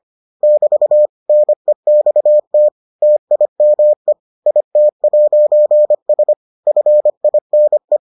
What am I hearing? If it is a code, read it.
Morse code, heard as a keyed tone, about 25 wpm, 600 Hz. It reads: <BT> NEXT TIME IT'S FINE